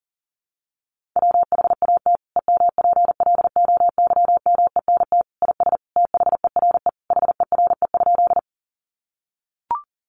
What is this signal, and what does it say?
Morse code, heard as a keyed tone, about 40 wpm, 700 Hz. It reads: WHAT EMPLOYMENT IS THERE HERE?
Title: Sentences from Top 1500 Words 40wpm